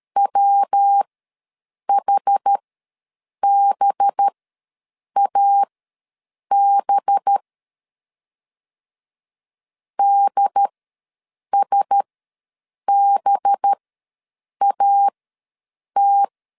Morse code, heard as a keyed tone, about 13 wpm, spaced as for 7 wpm, 800 Hz. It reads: WHBAB DSBAT
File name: 5 LB ws 12wpm